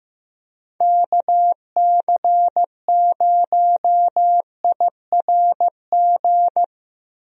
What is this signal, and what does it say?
Morse code, heard as a keyed tone, about 15 wpm, 700 Hz. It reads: KC0IRG
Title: USA Call Signs - 6 Characters - Repeated 2x - 15wpm